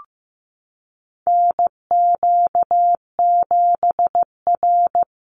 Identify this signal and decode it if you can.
Morse code, heard as a keyed tone, about 15 wpm, 700 Hz. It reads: NQ7R
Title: NA SOTA Chaser Callsigns 15wpm